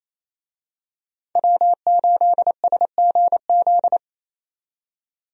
Morse code, heard as a keyed tone, about 28 wpm, 700 Hz. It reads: W8SGZ
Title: USA Call Signs - 5 Characters - 28wpm